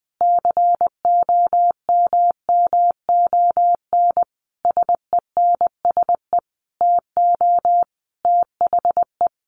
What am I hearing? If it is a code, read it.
Morse code, heard as a keyed tone, about 20 wpm, 700 Hz. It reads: COMMON SENSE TO THE
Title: Sentences from Top 1200 Words 20wpm